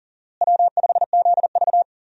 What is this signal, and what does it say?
Morse code, heard as a keyed tone, about 37 wpm, 700 Hz. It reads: W5ZV
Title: USA Call Signs - 4 Characters - Repeated 2x - 40wpm